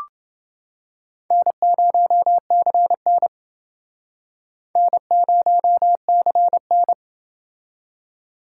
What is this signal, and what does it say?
Morse code, heard as a keyed tone, about 28 wpm, 700 Hz. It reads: N0CN N0CN